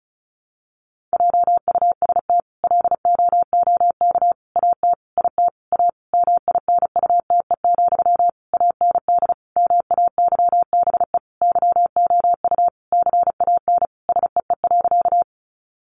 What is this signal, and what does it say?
Morse code, heard as a keyed tone, about 34 wpm, 700 Hz. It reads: JUST LOOK AT IT A MINUTE, AND MAYBE YOU CAN SEE.
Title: Sentences from Top 1000 Words 35wpm